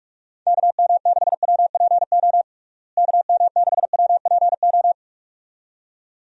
Morse code, heard as a keyed tone, about 41 wpm, 700 Hz. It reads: KM6WPO KM6WPO
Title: USA Call Signs - 6 Characters - Repeated 2x - 45wpm